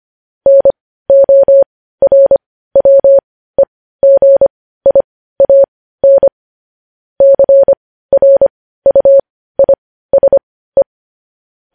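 Morse code, heard as a keyed tone, about 25 wpm, spaced as for 15 wpm, 550 Hz. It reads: NORWEGIAN CRUISE
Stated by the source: Morse practice sending news headlines at 15 wpm.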